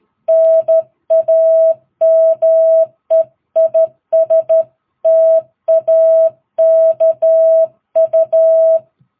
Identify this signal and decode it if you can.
Morse code, heard as a keyed tone, about 13 wpm, 660 Hz. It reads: NAME IS TAKU
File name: jh8cbh / 2022cw